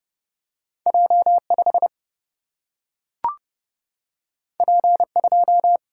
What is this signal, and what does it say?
Morse code, heard as a keyed tone, about 31 wpm, 700 Hz. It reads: J 5 P 2
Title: Single-Letter Single-Number 30wpm